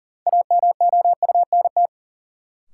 Morse code, heard as a keyed tone, about 40 wpm, 700 Hz. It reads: AMOUNT
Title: Sentences from Top 700 Words - New Word Drill - 40wpm